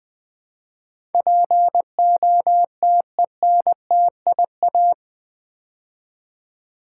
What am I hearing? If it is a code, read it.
Morse code, heard as a keyed tone, about 20 wpm, 700 Hz. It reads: POTENTIA
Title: Build A Word - 4 Letters At A Time - 11-Letter Words - 20wpm